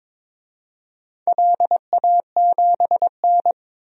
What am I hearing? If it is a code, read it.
Morse code, heard as a keyed tone, about 22 wpm, 700 Hz. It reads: LA7N
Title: DXCC-100 Call Signs - 4 Characters - Repeated 3x - 22wpm